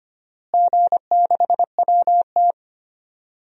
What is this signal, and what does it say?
Morse code, heard as a keyed tone, about 25 wpm, 700 Hz. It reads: G6WT
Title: DXCC-100 Call Signs - 4 Characters - Repeated 2x - 25wpm